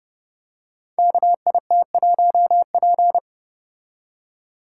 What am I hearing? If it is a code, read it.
Morse code, heard as a keyed tone, about 29 wpm, 700 Hz. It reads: KIT1P
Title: Code Groups - 5 Letter:Numbers - 30wpm